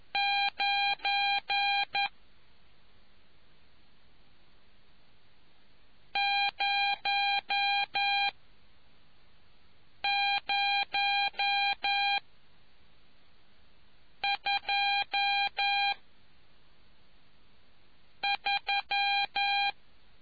Morse code, heard as a keyed tone, about 11 wpm, spaced as for 4 wpm, 800 Hz. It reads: 9 00 2 3